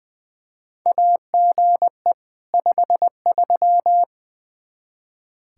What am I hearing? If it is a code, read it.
Morse code, heard as a keyed tone, about 20 wpm, 700 Hz. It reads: AGE 53
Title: Practice QSO - Exchange 3 - More Info - 20wpm